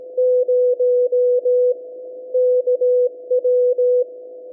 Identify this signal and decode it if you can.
Morse code, heard as a keyed tone, about 17 wpm, 500 Hz. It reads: 0 KW
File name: SAQ 17.20 cw